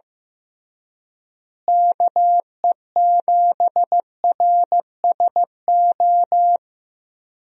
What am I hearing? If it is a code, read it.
Morse code, heard as a keyed tone, about 16 wpm, 700 Hz. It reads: KE7RSO
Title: USA Call Signs - 6 Characters - Repeated 3x - 15wpm